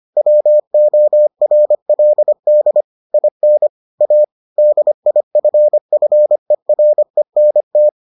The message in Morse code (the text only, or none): WORLD IN A DIFFERENT